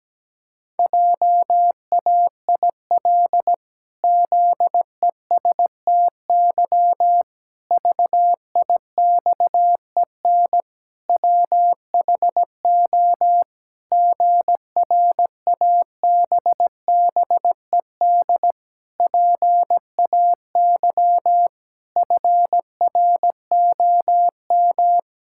Morse code, heard as a keyed tone, about 17 wpm, 700 Hz. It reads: JAIL ZESTY VIXEN WHO GRABBED PAY FROM
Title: Pangram Challenge v2 - 17wpm